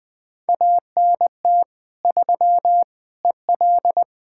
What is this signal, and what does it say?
Morse code, heard as a keyed tone, about 20 wpm, 700 Hz. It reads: ANT 3 EL
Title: Practice QSO - Exchange 2 - Info - 20wpm